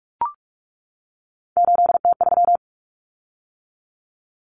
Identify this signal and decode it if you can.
Morse code, heard as a keyed tone, about 44 wpm, 700 Hz. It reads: 8T3
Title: Common Cut Numbers - 3-Digit 45wpm